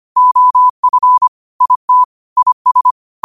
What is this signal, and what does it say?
Morse code, heard as a keyed tone, about 25 wpm, 1000 Hz. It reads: OF IT IS
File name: qotdfiles / 25wpm